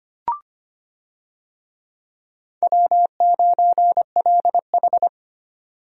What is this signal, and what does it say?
Morse code, heard as a keyed tone, about 25 wpm, 700 Hz. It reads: W9LH
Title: USA Call Signs - 4 Characters - 25wpm